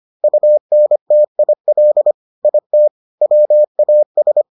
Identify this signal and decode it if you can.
Morse code, heard as a keyed tone, about 25 wpm, 600 Hz. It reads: UNTIL IT WAS